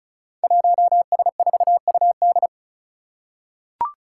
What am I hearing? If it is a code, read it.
Morse code, heard as a keyed tone, about 36 wpm, 700 Hz. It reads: 1S4UD
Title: Five Challenging Letter-Number Combinations 35wpm